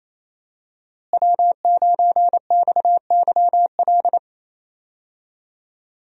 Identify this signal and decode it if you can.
Morse code, heard as a keyed tone, about 28 wpm, 700 Hz. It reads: W9XYL
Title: USA Call Signs - 5 Characters - Repeated 2x - 28wpm